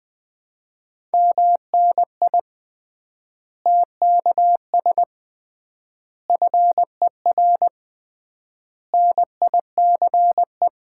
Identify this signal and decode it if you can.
Morse code, heard as a keyed tone, about 20 wpm, 700 Hz. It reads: MNI TKS FER NICE
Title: Common QSO Phrases - 3x Word-Spacing - 20wpm